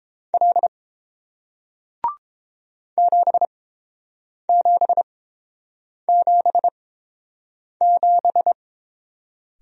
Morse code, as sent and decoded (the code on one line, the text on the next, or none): .-.. / --... --... --... --...
L 7777